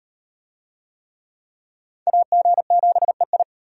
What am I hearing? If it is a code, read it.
Morse code, heard as a keyed tone, about 38 wpm, 700 Hz. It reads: AG7EI